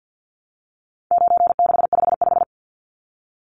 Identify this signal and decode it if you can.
Morse code, heard as a keyed tone, about 50 wpm, 700 Hz. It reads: 9655
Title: Numbers - 4-Digit 50wpm